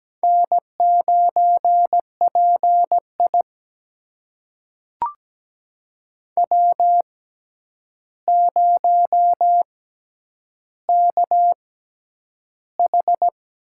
Morse code, heard as a keyed tone, about 17 wpm, spaced as for 9 wpm, 700 Hz. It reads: N9PI W0KH
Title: USA Call Signs - 4 Characters - 6x Character Spacing - 17wpm